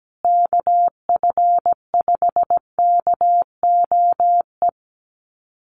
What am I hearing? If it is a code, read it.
Morse code, heard as a keyed tone, about 17 wpm, 700 Hz. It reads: KF5KOE
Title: USA Call Signs Repeated 3x 17wpm